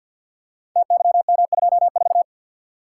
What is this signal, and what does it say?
Morse code, heard as a keyed tone, about 50 wpm, 700 Hz. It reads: TYM24
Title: Code Groups - 5 Letter:Numbers - 50wpm